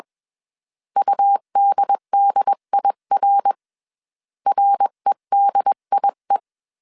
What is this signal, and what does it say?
Morse code, heard as a keyed tone, about 21 wpm, 800 Hz. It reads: UDDIR REDIE
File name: CW / DET / 1 3 1500C S B 20wpm 1754